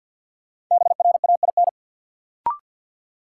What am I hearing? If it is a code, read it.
Morse code, heard as a keyed tone, about 52 wpm, 700 Hz. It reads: BRAIN
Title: Sentences from Top 1100 Words - New Word Drill Alt - 50wpm